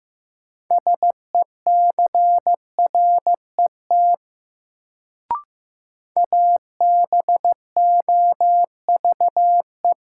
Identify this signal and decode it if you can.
Morse code, heard as a keyed tone, about 15 wpm, 700 Hz. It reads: SECRET ABOVE
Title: Sentences from Top 1200 Words - New Word Drill Alt - 15wpm